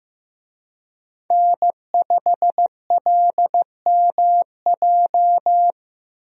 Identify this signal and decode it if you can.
Morse code, heard as a keyed tone, about 16 wpm, 700 Hz. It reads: N5LMJ
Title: USA Call Signs - 5 Characters - Repeated 3x - 15wpm